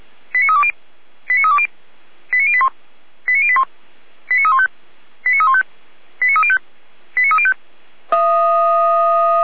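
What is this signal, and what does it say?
Five-tone selective call (ZVEI), heard as quick runs of five tones, tones of about 70 ms. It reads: 89320 89320 89081 89081 89315 89315 89305 89305
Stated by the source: ZVEI five-tone selective call received on an SDR.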